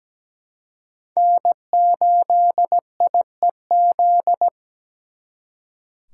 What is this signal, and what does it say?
Morse code, heard as a keyed tone, about 20 wpm, 700 Hz. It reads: N8IEZ
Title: Speed Racing - USA Call Signs 17wpm